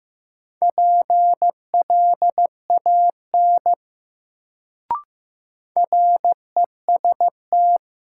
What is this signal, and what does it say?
Morse code, heard as a keyed tone, about 15 wpm, 700 Hz. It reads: PLAN REST
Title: Sentences from Top 500 Words - New Word Drill - 15wpm